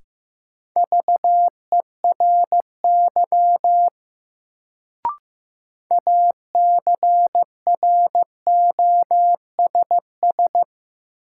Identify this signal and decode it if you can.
Morse code, heard as a keyed tone, about 15 wpm, 700 Hz. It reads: VERY ACROSS
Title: Sets of 2 Words - New Word Drill - 15wpm